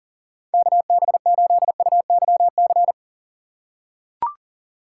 Morse code, heard as a keyed tone, about 40 wpm, 700 Hz. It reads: KB8UYC
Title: USA Call Signs - 6 Characters - 4x Character Spacing - 40wpm